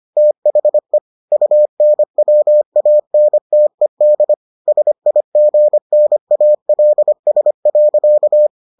Morse code, heard as a keyed tone, about 25 wpm, 600 Hz. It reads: THE UNWANTED SIGNALS.